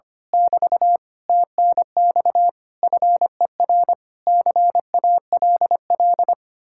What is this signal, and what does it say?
Morse code, heard as a keyed tone, about 24 wpm, 700 Hz. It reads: <BT> TNX FER CALL
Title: Speed-Racing Common QSO Phrases - 25wpm